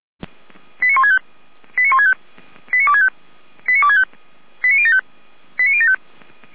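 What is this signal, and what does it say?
Five-tone selective call (ZVEI), heard as quick runs of five tones, tones of about 70 ms. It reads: 89165 89165 89265 89265 89075 89075
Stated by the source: ZVEI five-tone selective call 89073, received by SDR for feuerwehr-orlamuende fire service.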